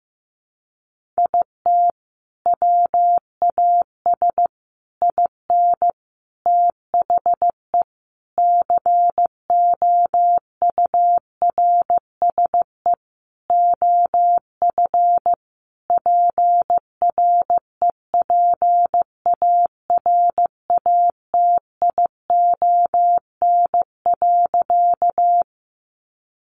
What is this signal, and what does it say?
Morse code, heard as a keyed tone, about 15 wpm, 700 Hz. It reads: IT WAS IN THE COURSE OF PREPARATION.
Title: Sentences from Top 1400 Words 15wpm